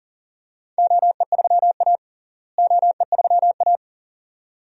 Morse code, heard as a keyed tone, about 40 wpm, 700 Hz. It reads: OE3A OE3A
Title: DXCC-100 Call Signs - 4 Characters - Repeated 2x - 40wpm